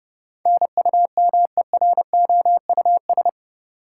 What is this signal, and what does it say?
Morse code, heard as a keyed tone, about 32 wpm, 700 Hz. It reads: NUMEROUS
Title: Sentences from Top 1400 Words - New Word Drill Alt - 30wpm